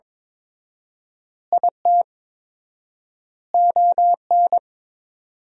Morse code, heard as a keyed tone, about 21 wpm, 700 Hz. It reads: IT ON
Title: 4x Word-Spacing - Sentences from Top 400 Words - 22wpm